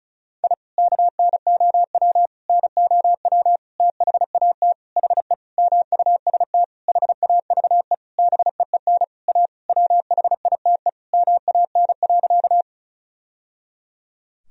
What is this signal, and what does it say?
Morse code, heard as a keyed tone, about 34 wpm, 690 Hz. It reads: I KNOW NOW THAT HE MUST HAVE BEEN A WHITE MAN.
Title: Sentences from Top 1000 Words - Review Part 1 - 35wpm